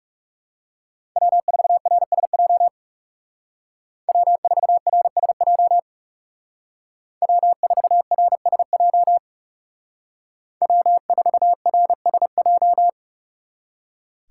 Morse code, heard as a keyed tone, about 37 wpm, 700 Hz. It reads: W4RSJ W4RSJ W4RSJ W4RSJ
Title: Speed Racing - USA 1x3 Call Signs 30wpm